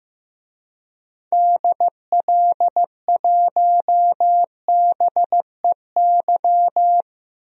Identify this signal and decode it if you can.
Morse code, heard as a keyed tone, about 15 wpm, 700 Hz. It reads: DL1BEY
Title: DXCC-100 Call Signs - 6 Characters - Repeated 2x - 15wpm